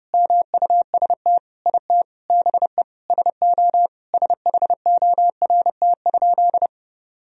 Morse code, heard as a keyed tone, about 30 wpm, 700 Hz. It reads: MUST IT BE SO SHORT?
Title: Sentences from Top 1000 Words - Review Part 2 - 30wpm